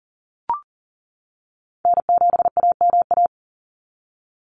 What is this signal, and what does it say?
Morse code, heard as a keyed tone, about 40 wpm, 700 Hz. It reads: N7AMA
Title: NA SOTA Chaser Callsigns 40wpm